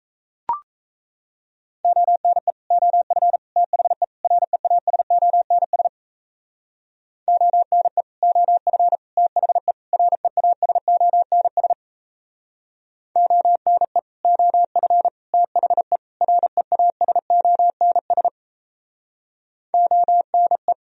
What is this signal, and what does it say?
Morse code, heard as a keyed tone, about 33 wpm, 700 Hz. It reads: ONE OF THE REASONS ONE OF THE REASONS ONE OF THE REASONS ONE
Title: Speed-Racing - Sets of 4 Words 28wpm